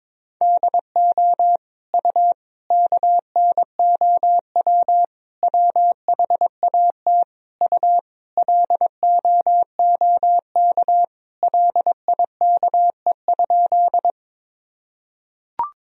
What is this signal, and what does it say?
Morse code, heard as a keyed tone, about 22 wpm, 700 Hz. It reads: DO U KNOW WHAT U LOOK LIKE?
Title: Sentences from Top 100 Words and Abbreviations 22wpm